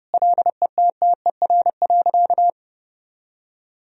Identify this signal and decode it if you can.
Morse code, heard as a keyed tone, about 32 wpm, 700 Hz. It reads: LETTER.
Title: Sentences from Top 1000 Words - Review Part 1 - 30wpm